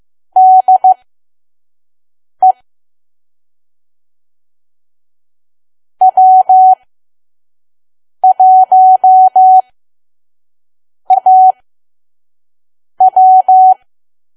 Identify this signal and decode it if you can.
Morse code, heard as a keyed tone, about 15 wpm, spaced as for 5 wpm, 750 Hz. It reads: DE W1AW
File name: Archive / 5 WPM files / PM